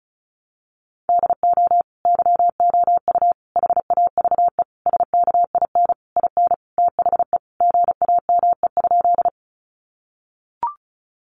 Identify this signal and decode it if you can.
Morse code, heard as a keyed tone, about 34 wpm, 700 Hz. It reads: DO YOU HAVE SKIN IN THE GAME?
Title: Sentences from Top 1300 Words 35wpm